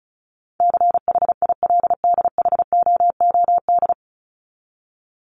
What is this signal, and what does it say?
Morse code, heard as a keyed tone, about 34 wpm, 700 Hz. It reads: CHILDHOOD
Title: Top 1751-2000 Words 35wpm